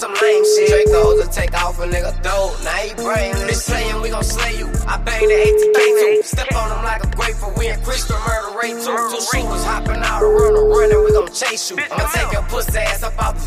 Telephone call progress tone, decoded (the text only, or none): ringback tone